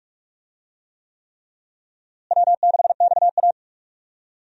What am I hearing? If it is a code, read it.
Morse code, heard as a keyed tone, about 42 wpm, 700 Hz. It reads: W6XA
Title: USA Call Signs - 4 Characters - Repeated 3x - 45wpm